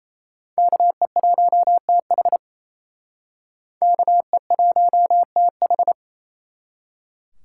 Morse code, heard as a keyed tone, about 36 wpm, 700 Hz. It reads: KE1TH KE1TH